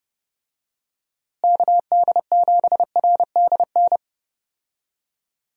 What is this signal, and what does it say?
Morse code, heard as a keyed tone, about 30 wpm, 700 Hz. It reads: KD7RDN
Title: USA Call Signs - 6 Characters - Repeated 3x - 30wpm